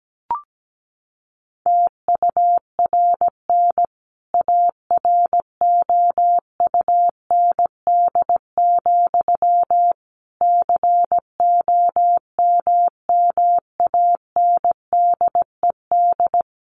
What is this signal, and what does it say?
Morse code, heard as a keyed tone, about 17 wpm, 700 Hz. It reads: TURN AROUND, COMMANDED
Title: A Sound Of Thunder 17wpm